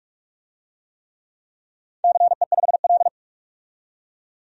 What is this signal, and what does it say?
Morse code, heard as a keyed tone, about 45 wpm, 700 Hz. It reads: CE5L
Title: DXCC-100 Call Signs - 4 Characters - 45wpm